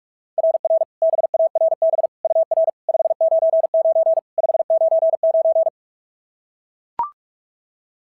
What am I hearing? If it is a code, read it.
Morse code, heard as a keyed tone, about 44 wpm, 690 Hz. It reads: RR BARB UR 599 599
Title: SOTA QSO Practice 45wpm